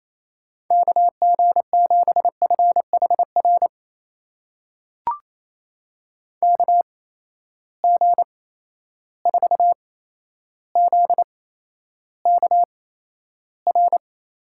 Morse code, heard as a keyed tone, about 28 wpm, spaced as for 12 wpm, 700 Hz. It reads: KG7FHR KG4ZKR
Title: USA Call Signs - 6 Characters - 8x Character Spacing - 28wpm